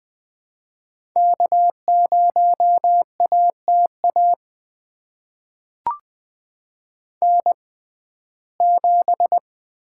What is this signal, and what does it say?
Morse code, heard as a keyed tone, about 21 wpm, spaced as for 15 wpm, 700 Hz. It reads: K0ATA N7
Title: USA Call Signs - 5 Characters - 6x Character Spacing - 20wpm